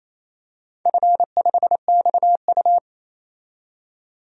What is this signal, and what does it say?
Morse code, heard as a keyed tone, about 28 wpm, 700 Hz. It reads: F5XU
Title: DXCC-100 Call Signs - 4 Characters - 28wpm